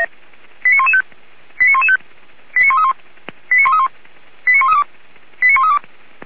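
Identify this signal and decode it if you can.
Five-tone selective call (ZVEI), heard as quick runs of five tones, tones of about 80 ms. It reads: 89115 89115 89131 89131 89132 89132
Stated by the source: ZVEI five-tone call 89110 received by SDR.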